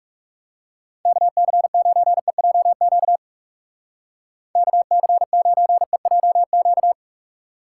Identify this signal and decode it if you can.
Morse code, heard as a keyed tone, about 36 wpm, 700 Hz. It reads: KC9EJQ KC9EJQ
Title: Speed Racing - USA Call Signs 30wpm